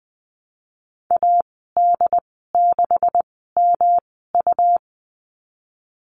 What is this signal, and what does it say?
Morse code, heard as a keyed tone, about 20 wpm, spaced as for 15 wpm, 700 Hz. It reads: AD6MU
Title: 147 Lesson 24g - Letter y - Call Signs 20wpm